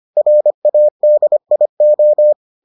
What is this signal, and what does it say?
Morse code, heard as a keyed tone, about 25 wpm, 600 Hz. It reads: RADIO